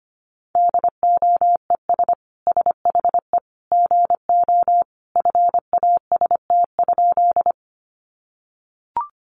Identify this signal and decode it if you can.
Morse code, heard as a keyed tone, about 25 wpm, 700 Hz. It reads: DOES SHE GO FAST?
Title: Sentences from Top 400 Words 25wpm